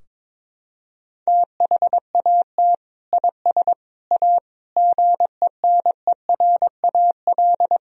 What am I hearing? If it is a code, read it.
Morse code, heard as a keyed tone, about 22 wpm, 700 Hz. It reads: THAT IS A GENERAL
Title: Sentences from Top 1000 Words - Review Part 3 - 22wpm